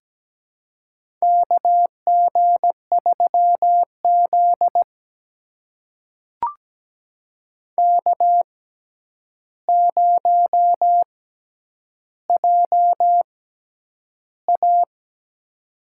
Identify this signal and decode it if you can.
Morse code, heard as a keyed tone, about 17 wpm, spaced as for 9 wpm, 700 Hz. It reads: KG3Z K0JA
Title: USA Call Signs - 4 Characters - 6x Character Spacing - 17wpm